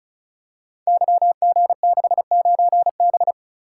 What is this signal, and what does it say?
Morse code, heard as a keyed tone, about 34 wpm, 700 Hz. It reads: YG69B
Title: Code Groups - 5 Letter:Numbers - 35wpm